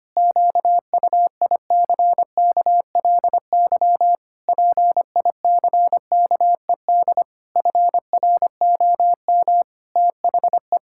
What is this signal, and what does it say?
Morse code, heard as a keyed tone, about 25 wpm, 700 Hz. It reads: QUICKLY PICKED FROM THE
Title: Pangram Challenge v2 - 25wpm